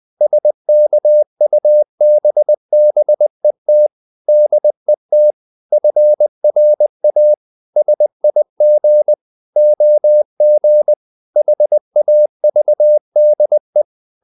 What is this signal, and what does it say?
Morse code, heard as a keyed tone, about 20 wpm, 600 Hz. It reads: SKUBBET DET FRA SIG OG HAVDE
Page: Ferie 20 wpm